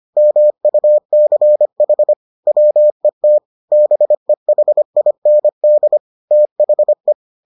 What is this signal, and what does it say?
Morse code, heard as a keyed tone, about 25 wpm, 600 Hz. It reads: MUCH WET BEHIND THE